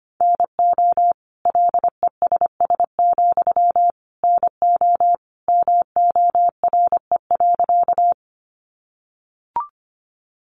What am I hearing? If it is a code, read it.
Morse code, heard as a keyed tone, about 25 wpm, 700 Hz. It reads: NO LESS, NO MORE.
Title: Sentences from Top 400 Words 25wpm